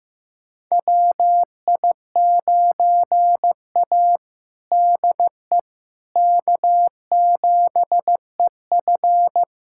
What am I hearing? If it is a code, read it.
Morse code, heard as a keyed tone, about 15 wpm, 700 Hz. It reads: WI9A DE K7EF
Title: USA Callsign DE Callsign 15wpm